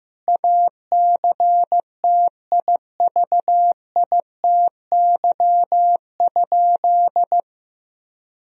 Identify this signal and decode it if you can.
Morse code, heard as a keyed tone, about 15 wpm, 700 Hz. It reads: ACTIVITY?
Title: Sentences from Top 1000 Words - Review Part 4 - 15wpm